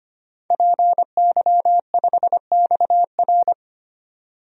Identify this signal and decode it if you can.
Morse code, heard as a keyed tone, about 25 wpm, 700 Hz. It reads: PY5XR
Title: DXCC-100 Call Signs - 5 Characters - 25wpm